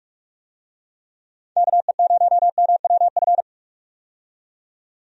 Morse code, heard as a keyed tone, about 46 wpm, 700 Hz. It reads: KE0MWF
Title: USA Call Signs - 6 Characters - 45wpm